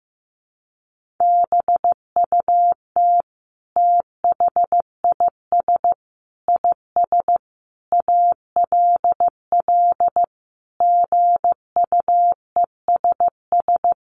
Morse code, heard as a keyed tone, about 15 wpm, 700 Hz. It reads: BUT THIS IS ALL GUESS
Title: Sentences from Top 1500 Words 15wpm